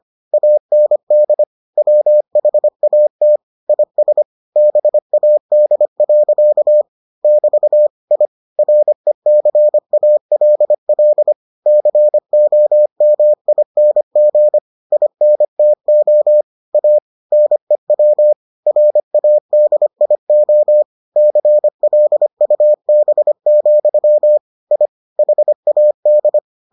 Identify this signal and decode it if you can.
Morse code, heard as a keyed tone, about 25 wpm, 600 Hz. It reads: AND WHAT IS BAD. <BT> I RECALL COMING INTO A NEW RADIO CLUB, I HAD